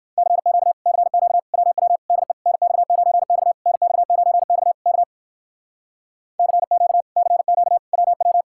Morse code, heard as a keyed tone, about 56 wpm, 700 Hz. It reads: CQ CQ WW DE NX9X NX9X K CQ CQ WW
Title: Speed-Racing Common QSO Phrases - 45wpm